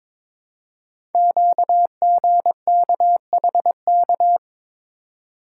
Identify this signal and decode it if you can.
Morse code, heard as a keyed tone, about 22 wpm, 700 Hz. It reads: QGKHK
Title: Code Groups - 5 Letters - 22wpm